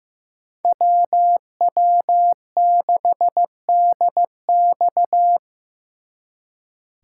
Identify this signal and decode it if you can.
Morse code, heard as a keyed tone, about 18 wpm, 700 Hz. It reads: WW6DX
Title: Speed Racing - USA 2x2 Call Signs 15wpm